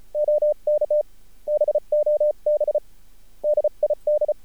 Morse code, heard as a keyed tone, about 51 wpm, spaced as for 36 wpm, 600 Hz. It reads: OK BOB DID
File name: archived / CRSnet